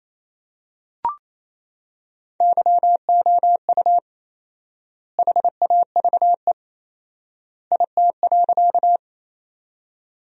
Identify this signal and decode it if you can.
Morse code, heard as a keyed tone, about 28 wpm, 700 Hz. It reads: YOU HAVE IT.
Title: 4x Word-Spacing - Sentences from Top 100 Words - 28wpm